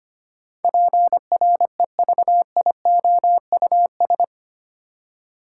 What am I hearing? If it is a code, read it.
Morse code, heard as a keyed tone, about 25 wpm, 690 Hz. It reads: PREVIOUS
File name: 8 Letter Words - First 3 Letters Given - 25wpm